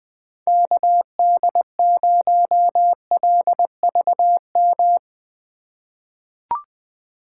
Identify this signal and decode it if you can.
Morse code, heard as a keyed tone, about 20 wpm, 700 Hz. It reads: KD0LVM
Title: USA Call Signs - 6 Characters - 2x Character Spacing - 20wpm